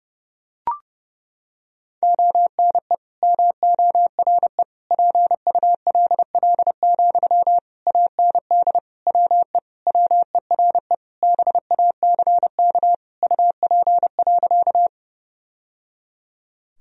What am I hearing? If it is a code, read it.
Morse code, heard as a keyed tone, about 30 wpm, 700 Hz. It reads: ONE MORE PULL, AND WE WERE BACK UP.
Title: Sentences from Top 1000 Words - Review Part 2 - 30wpm